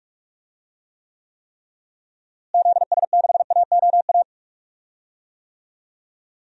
Morse code, heard as a keyed tone, about 46 wpm, 700 Hz. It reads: ZS6AOA
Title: DXCC-100 Call Signs - 6 Characters - Repeated 3x - 45wpm